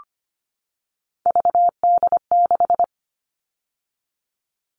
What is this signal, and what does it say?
Morse code, heard as a keyed tone, about 25 wpm, 700 Hz. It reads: VD6
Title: All Cut Numbers - 3-Digit 25wpm